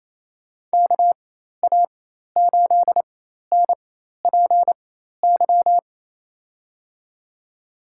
Morse code, heard as a keyed tone, about 28 wpm, spaced as for 13 wpm, 700 Hz. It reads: KA8NPY
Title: USA Call Signs - 4x Character Spacing - 28wpm